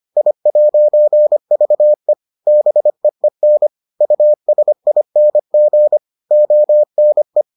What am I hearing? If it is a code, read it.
Morse code, heard as a keyed tone, about 25 wpm, 600 Hz. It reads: I'VE BEEN USING ONE